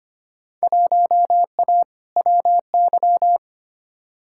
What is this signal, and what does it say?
Morse code, heard as a keyed tone, about 25 wpm, 700 Hz. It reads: 1A WY
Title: Field Day Exchanges - 25wpm